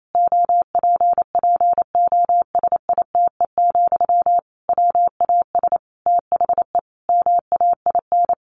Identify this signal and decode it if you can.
Morse code, heard as a keyed tone, about 28 wpm, 700 Hz. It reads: OPPOSITE, WAS THE MAIN
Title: Sentences from Top 1300 Words 28wpm